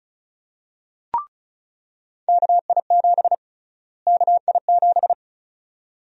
Morse code, heard as a keyed tone, about 36 wpm, 700 Hz. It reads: KI7 KI7
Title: USA Call Sign Prefixes - Repeated 2x - 35wpm